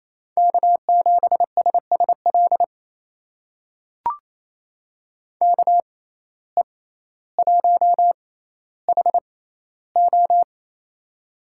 Morse code, heard as a keyed tone, about 28 wpm, spaced as for 14 wpm, 700 Hz. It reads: K7SSL KE1HO
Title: USA Call Signs - 5 Characters - 6x Character Spacing - 28wpm